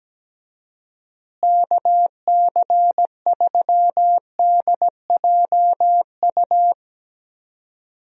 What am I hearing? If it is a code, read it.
Morse code, heard as a keyed tone, about 17 wpm, 700 Hz. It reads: KC3DJU
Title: USA Call Signs - 6 Characters - Repeated 3x - 17wpm